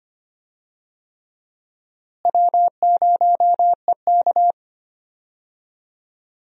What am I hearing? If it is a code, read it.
Morse code, heard as a keyed tone, about 25 wpm, 700 Hz. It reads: W0EK
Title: USA Call Signs - 4 Characters - Repeated 3x - 25wpm